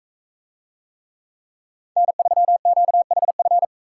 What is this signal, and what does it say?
Morse code, heard as a keyed tone, about 42 wpm, 700 Hz. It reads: N3QHF